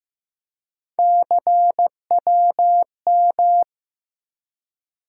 Morse code, heard as a keyed tone, about 15 wpm, 700 Hz. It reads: CWM
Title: USA Call Sign Postfixes - 3 Letters 15wpm